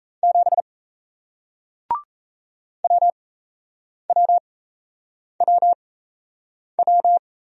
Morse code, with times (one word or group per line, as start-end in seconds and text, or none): 0.2-0.6 7
2.8-7.2 WWWW